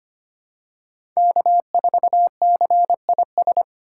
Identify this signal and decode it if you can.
Morse code, heard as a keyed tone, about 25 wpm, 700 Hz. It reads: K4CIS
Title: USA Call Signs - 5 Characters - 3x Character Spacing - 25wpm